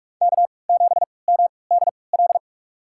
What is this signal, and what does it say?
Morse code, heard as a keyed tone, about 42 wpm, spaced as for 25 wpm, 700 Hz. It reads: K7MDL
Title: USA Call Signs - 5 Characters - 3x Character Spacing - 45wpm